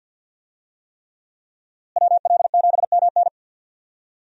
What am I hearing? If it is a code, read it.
Morse code, heard as a keyed tone, about 46 wpm, 700 Hz. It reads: WL7MN
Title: USA Call Signs - 5 Characters - Repeated 3x - 50wpm